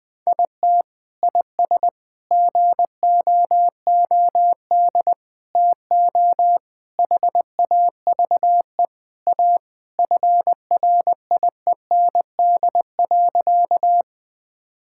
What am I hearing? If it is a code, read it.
Morse code, heard as a keyed tone, about 20 wpm, 700 Hz. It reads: IT IS GOOD TO HAVE A FRIEND.
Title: Sentences with 3-Word Phrases - 20wpm